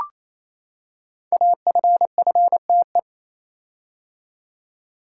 Morse code, heard as a keyed tone, about 29 wpm, 700 Hz. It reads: AFFTE
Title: Code Groups - 5 Letters - 28wpm